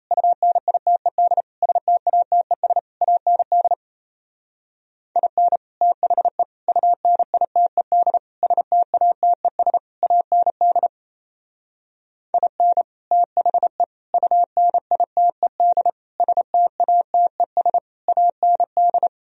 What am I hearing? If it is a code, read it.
Morse code, heard as a keyed tone, about 36 wpm, 700 Hz. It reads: UNITED STATES AND IN THE UNITED STATES AND IN THE UNITED STATES AND